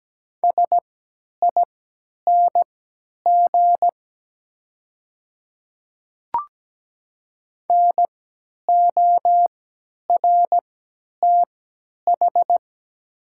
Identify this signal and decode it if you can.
Morse code, heard as a keyed tone, about 17 wpm, spaced as for 10 wpm, 700 Hz. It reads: SING NORTH
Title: 3x Character Spacing - Top 500 Words - 17wpm